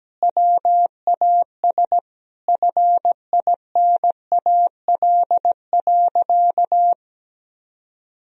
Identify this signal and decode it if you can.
Morse code, heard as a keyed tone, about 17 wpm, 700 Hz. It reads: WAS FINAL.
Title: Sentences from Top 1000 Words - Review Part 2 - 17wpm